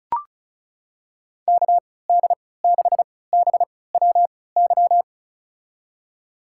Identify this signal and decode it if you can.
Morse code, heard as a keyed tone, about 36 wpm, spaced as for 20 wpm, 700 Hz. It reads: KD6BWY
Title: USA Call Signs - 3x Character Spacing - 35wpm